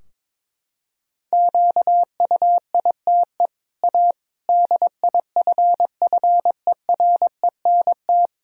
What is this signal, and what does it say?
Morse code, heard as a keyed tone, about 21 wpm, 700 Hz. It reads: QUITE A DIFFERENT
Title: Sentences from Top 1000 Words - Review Part 3 - 22wpm